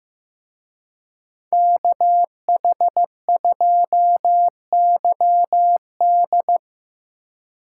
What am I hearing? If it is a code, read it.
Morse code, heard as a keyed tone, about 18 wpm, 700 Hz. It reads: KH2YD
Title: Speed Racing - USA 2x2 Call Signs 15wpm